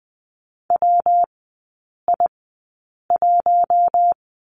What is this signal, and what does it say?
Morse code, heard as a keyed tone, about 20 wpm, spaced as for 8 wpm, 690 Hz. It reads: WI1
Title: 133 Lesson 22e - Question Mark - Three Characters 20wpm